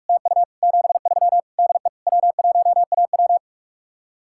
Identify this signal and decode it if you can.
Morse code, heard as a keyed tone, about 48 wpm, 700 Hz. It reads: TU 73 DE W1AW